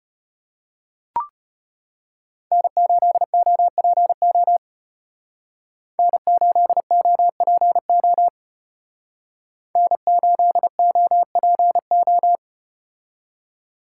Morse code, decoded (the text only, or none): N8OPO N8OPO N8OPO